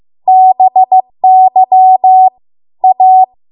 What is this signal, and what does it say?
Morse code, heard as a keyed tone, about 15 wpm, 750 Hz. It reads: BY A